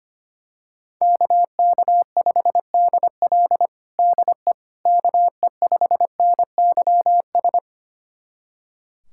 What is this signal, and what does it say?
Morse code, heard as a keyed tone, about 25 wpm, 700 Hz. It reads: KK5DL DE KE5NYS
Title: USA Callsign DE Callsign 25wpm